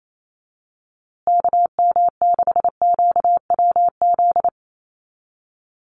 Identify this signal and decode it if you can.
Morse code, heard as a keyed tone, about 28 wpm, 700 Hz. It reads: KM6QWZ
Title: USA Call Signs Repeated 2x 28wpm